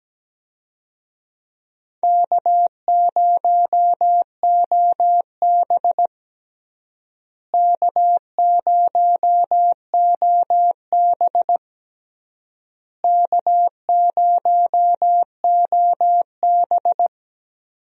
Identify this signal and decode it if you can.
Morse code, heard as a keyed tone, about 17 wpm, 700 Hz. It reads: K0OB K0OB K0OB
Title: USA Call Signs - 4 Characters - Repeated 3x - 17wpm